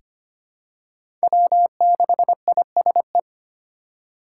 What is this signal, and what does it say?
Morse code, heard as a keyed tone, about 25 wpm, 700 Hz. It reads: W6ISE
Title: USA Call Signs - 5 Characters - 4x Character Spacing - 25wpm